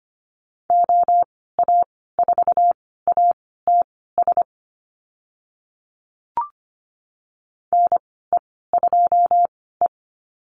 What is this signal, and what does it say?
Morse code, heard as a keyed tone, about 25 wpm, spaced as for 16 wpm, 700 Hz. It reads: OA4ATS NE2E
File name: Lesson 15g - Number 5 - Call Signs 25wpm